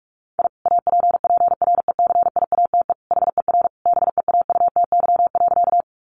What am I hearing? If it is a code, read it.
Morse code, heard as a keyed tone, about 44 wpm, 700 Hz. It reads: I APPRECIATE HER BEAUTY.
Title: Sentences from Top 1300 Words 45wpm